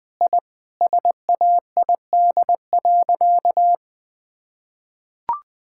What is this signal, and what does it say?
Morse code, heard as a keyed tone, about 20 wpm, 700 Hz. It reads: I SAID.
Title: Sentences with 2-Word Phrases - 20wpm